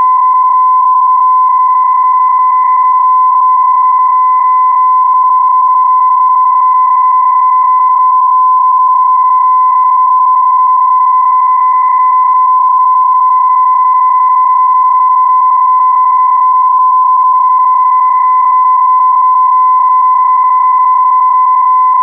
PSK31, about 31.25 baud.